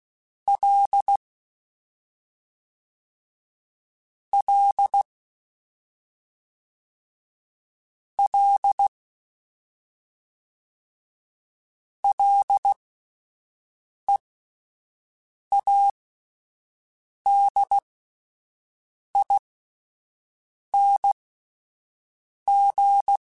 Morse code, heard as a keyed tone, about 16 wpm, spaced as for 5 wpm, 800 Hz. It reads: L L L LEADING
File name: Ls / Single L m3s / Single L Prefixe 3x2 Twice @5